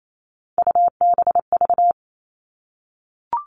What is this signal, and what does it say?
Morse code, heard as a keyed tone, about 28 wpm, 700 Hz. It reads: UBV
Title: All Cut Numbers - 3-Digit 28wpm